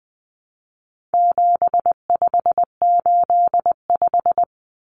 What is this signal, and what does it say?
Morse code, heard as a keyed tone, about 20 wpm, 700 Hz. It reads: 7585
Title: Common Cut Numbers - 4-Digit 20wpm